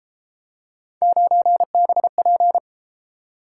Morse code, heard as a keyed tone, about 33 wpm, 700 Hz. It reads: 9BP